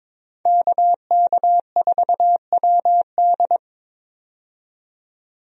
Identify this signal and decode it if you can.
Morse code, heard as a keyed tone, about 22 wpm, 700 Hz. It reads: KK4WD
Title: USA Call Signs - 5 Characters - 22wpm